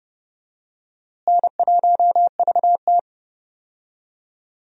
Morse code, heard as a keyed tone, about 32 wpm, 700 Hz. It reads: N1VT
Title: USA Call Signs - 4 Characters - Repeated 2x - 30wpm